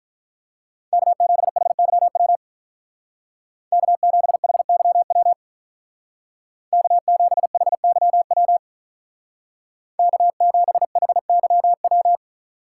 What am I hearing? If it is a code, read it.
Morse code, heard as a keyed tone, about 43 wpm, 700 Hz. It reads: K7HYW K7HYW K7HYW K7HYW
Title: Speed Racing - USA Call Signs 35wpm